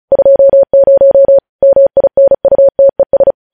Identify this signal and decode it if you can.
Morse code, heard as a keyed tone, about 36 wpm, 550 Hz. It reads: 20 MINUTES
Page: News headlines in Morse code at 35 wpm